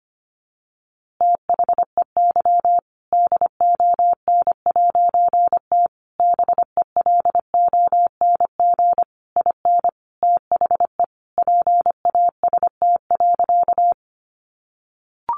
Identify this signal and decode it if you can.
Morse code, heard as a keyed tone, about 25 wpm, 700 Hz. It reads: THEY DON'T BELONG IN THE PAST.
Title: A Sound Of Thunder 25wpm